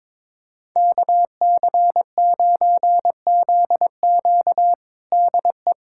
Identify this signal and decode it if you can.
Morse code, heard as a keyed tone, about 22 wpm, 700 Hz. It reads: KC9ZQ DE
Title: USA Callsign DE Callsign - 5 Characters - 22wpm